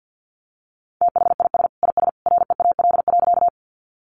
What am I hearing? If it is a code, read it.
Morse code, heard as a keyed tone, about 50 wpm, 700 Hz. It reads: THIS IS REAL.
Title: Sentences from Top 300 Words 50wpm